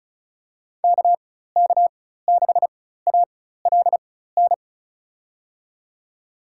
Morse code, heard as a keyed tone, about 36 wpm, spaced as for 16 wpm, 700 Hz. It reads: KK6ALN
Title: USA Call Signs - 6 Characters - 4x Character Spacing - 35wpm